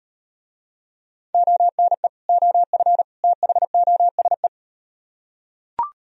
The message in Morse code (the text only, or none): ONE OF THOSE